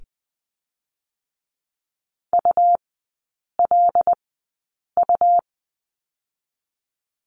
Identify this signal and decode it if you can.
Morse code, heard as a keyed tone, about 20 wpm, spaced as for 8 wpm, 700 Hz. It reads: ULU
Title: 127 Lesson 21e - Number 6 - Three Characters 20wpm